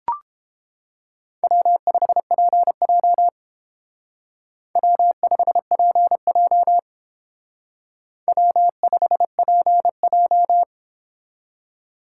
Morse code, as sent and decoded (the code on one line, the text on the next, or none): .-- ..... .--. .--- / .-- ..... .--. .--- / .-- ..... .--. .---
W5PJ W5PJ W5PJ